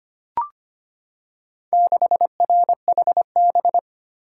Morse code, heard as a keyed tone, about 25 wpm, 700 Hz. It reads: 6RHB
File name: practice / 25wpm / Four Letter-Number Combinations - 25wpm